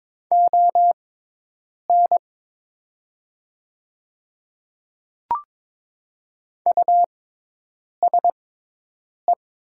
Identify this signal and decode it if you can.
Morse code, heard as a keyed tone, about 22 wpm, spaced as for 8 wpm, 700 Hz. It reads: ON USE
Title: 6x Character Spacing - Top 100 Words - 22wpm